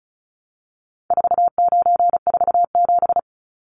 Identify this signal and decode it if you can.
Morse code, heard as a keyed tone, about 34 wpm, 700 Hz. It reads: 4947
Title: Numbers - 4-Digit 35wpm